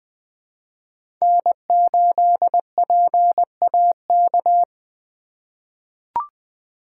Morse code, heard as a keyed tone, about 20 wpm, 700 Hz. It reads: N8PAK
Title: USA Call Signs - 5 Characters - 3x Character Spacing - 20wpm